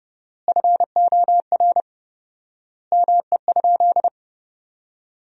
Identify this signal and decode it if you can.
Morse code, heard as a keyed tone, about 28 wpm, 700 Hz. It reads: FOR ME?
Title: 4x Word-Spacing - Sentences from Top 500 Words - 30wpm